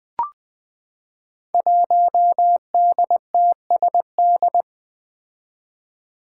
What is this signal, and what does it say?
Morse code, heard as a keyed tone, about 20 wpm, 700 Hz. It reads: 1DTSD
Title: Five Challenging Letter-Number Combinations 20wpm